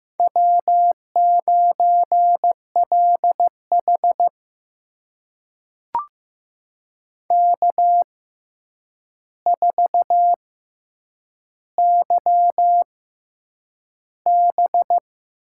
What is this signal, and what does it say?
Morse code, heard as a keyed tone, about 15 wpm, spaced as for 8 wpm, 700 Hz. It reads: W9LH K4YB
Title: USA Call Signs - 4 Characters - 6x Character Spacing - 15wpm